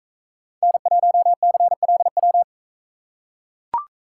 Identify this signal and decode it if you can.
Morse code, heard as a keyed tone, about 44 wpm, 700 Hz. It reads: N1CLW